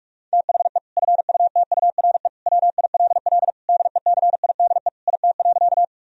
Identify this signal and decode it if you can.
Morse code, heard as a keyed tone, about 41 wpm, 700 Hz. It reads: THE FUTURE WILL DECIDE IT.
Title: Sentences from Top 1000 Words - Review Part 3 - 45wpm